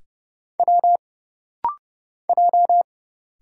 Morse code, heard as a keyed tone, about 30 wpm, 700 Hz. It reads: W J
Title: Single Letters - Rapid-Fire 30wpm